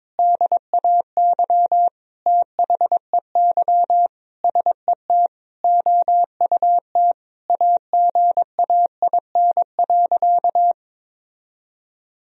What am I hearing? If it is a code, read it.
Morse code, heard as a keyed tone, about 22 wpm, 700 Hz. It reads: DAY THEY SET OUT AGAIN.
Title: Sentences from Top 1000 Words - Review Part 4 - 22wpm